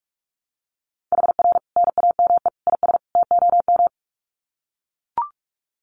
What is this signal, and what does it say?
Morse code, heard as a keyed tone, about 44 wpm, 700 Hz. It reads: HR NAME IS TOM
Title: Common QSO Phrases 45wpm